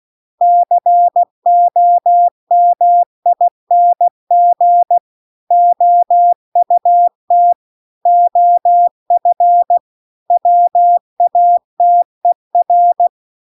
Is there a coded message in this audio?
Yes: Morse code from 0.4 to 13.1 s, heard as a keyed tone, about 16 wpm, 700 Hz. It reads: COMING OUT OF WATER